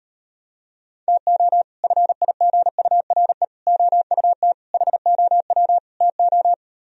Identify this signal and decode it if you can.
Morse code, heard as a keyed tone, about 38 wpm, 700 Hz. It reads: TO FIGURE OUT HOW TO